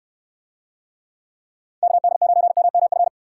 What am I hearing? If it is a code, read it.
Morse code, heard as a keyed tone, about 69 wpm, 700 Hz. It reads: KD9MMF